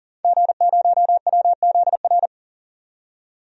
Morse code, heard as a keyed tone, about 40 wpm, 700 Hz. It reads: G0WZR
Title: DXCC-100 Call Signs - 5 Characters - 40wpm